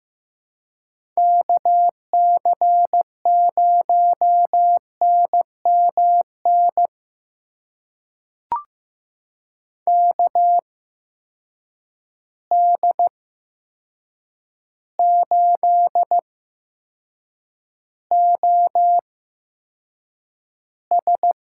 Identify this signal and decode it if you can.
Morse code, heard as a keyed tone, about 15 wpm, spaced as for 7 wpm, 700 Hz. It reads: KC0NMN KD8OS
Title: USA Call Signs - 6 Characters - 8x Character Spacing - 15wpm